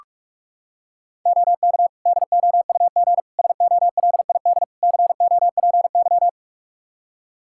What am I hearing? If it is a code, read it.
Morse code, heard as a keyed tone, about 41 wpm, 700 Hz. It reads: OK DOUG SOLID COPY
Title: Practice QSO - Exchange 3 - More Info - 45wpm